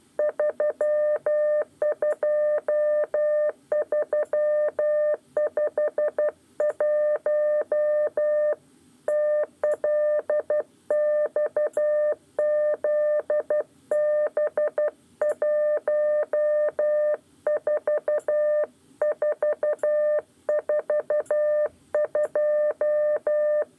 Morse code, heard as a keyed tone, about 12 wpm, 600 Hz. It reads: 32351 TLXZB 14442
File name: 8 lezione cw 1